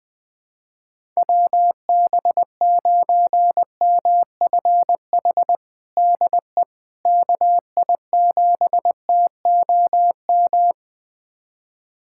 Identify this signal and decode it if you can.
Morse code, heard as a keyed tone, about 20 wpm, 700 Hz. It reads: WB9MFH DE KI7TOM
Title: USA Callsign DE Callsign - 6 Characters - 20wpm